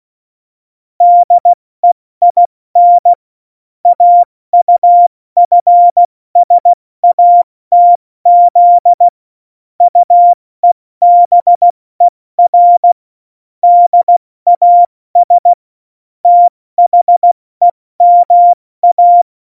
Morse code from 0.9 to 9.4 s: DEIN AUFSATZ